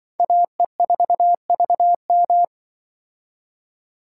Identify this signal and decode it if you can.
Morse code, heard as a keyed tone, about 25 wpm, 700 Hz. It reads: AE4VM
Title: Speed Racing - USA 2x2 Call Signs 20wpm